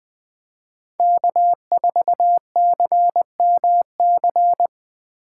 Morse code, heard as a keyed tone, about 21 wpm, 700 Hz. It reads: K4CMC
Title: Speed Racing - USA 1x3 Call Signs 17wpm